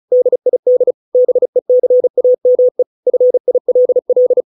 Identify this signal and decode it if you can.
Morse code, heard as a keyed tone, about 35 wpm, 500 Hz. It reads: DID BECAME FILL